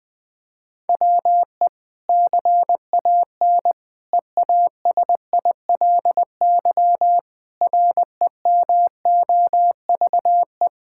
Morse code, heard as a keyed tone, about 20 wpm, 700 Hz. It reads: WE CAN EASILY REMOVE
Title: Sentences from Top 1000 Words - Review Part 5 - 20wpm